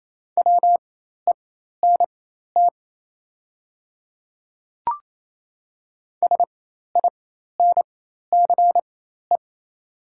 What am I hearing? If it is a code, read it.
Morse code, heard as a keyed tone, about 28 wpm, spaced as for 13 wpm, 700 Hz. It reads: WENT SINCE
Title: 4x Character Spacing - Top 300 Words - 28wpm